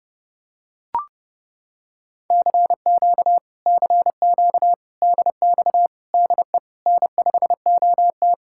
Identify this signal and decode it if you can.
Morse code, heard as a keyed tone, about 28 wpm, 700 Hz. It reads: CQ CQ DX DE N5OT